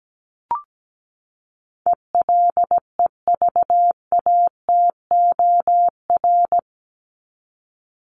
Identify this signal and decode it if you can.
Morse code, heard as a keyed tone, about 17 wpm, 700 Hz. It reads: ELEVATOR
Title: Top 1501-1750 Words 17wpm